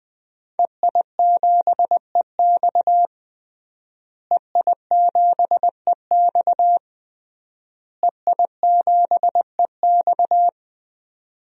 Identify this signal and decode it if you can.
Morse code, heard as a keyed tone, about 20 wpm, 700 Hz. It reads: EI7EX EI7EX EI7EX
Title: DXCC-100 Call Signs - 5 Characters - Repeated 3x - 20wpm